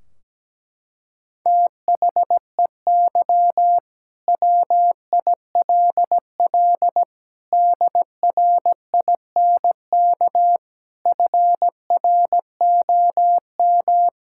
Morse code, heard as a keyed tone, about 17 wpm, 700 Hz. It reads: THEY WILL DRINK FROM
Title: Sentences from Top 1600 Words 17wpm